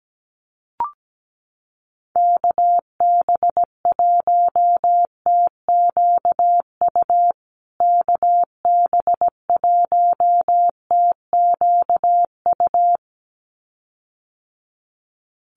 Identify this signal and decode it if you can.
Morse code, heard as a keyed tone, about 17 wpm, 700 Hz. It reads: KB1TQU KB1TQU
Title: USA Call Signs Repeated 2x 17wpm